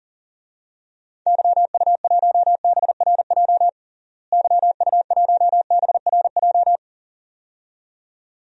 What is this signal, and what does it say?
Morse code, heard as a keyed tone, about 40 wpm, 700 Hz. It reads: YU1BRJ YU1BRJ
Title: DXCC-100 Call Signs - 6 Characters - Repeated 2x - 40wpm